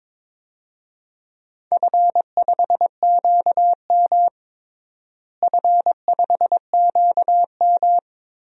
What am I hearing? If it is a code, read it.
Morse code, heard as a keyed tone, about 21 wpm, 700 Hz. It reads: F5QM F5QM
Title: DXCC-100 Call Signs - 4 Characters - Repeated 3x - 22wpm